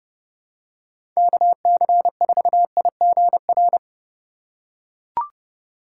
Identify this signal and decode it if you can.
Morse code, heard as a keyed tone, about 30 wpm, 700 Hz. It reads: KC4IGR
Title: USA Call Signs - 6 Characters - 4x Character Spacing - 30wpm